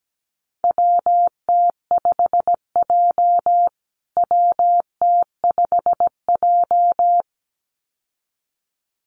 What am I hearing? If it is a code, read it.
Morse code, heard as a keyed tone, about 17 wpm, 700 Hz. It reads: WT5J WT5J
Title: USA Call Signs Repeated 2x 17wpm